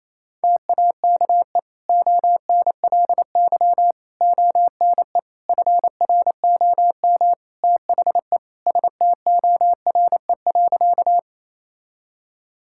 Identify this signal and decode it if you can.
Morse code, heard as a keyed tone, about 28 wpm, 700 Hz. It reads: TAKE ONLY ONE FROM THE STORE.
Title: Sentences from Top 1000 Words - Review Part 5 - 28wpm